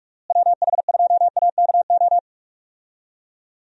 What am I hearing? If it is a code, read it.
Morse code, heard as a keyed tone, about 46 wpm, 700 Hz. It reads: WH2AKO
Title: USA Call Signs - 6 Characters - Repeated 2x - 45wpm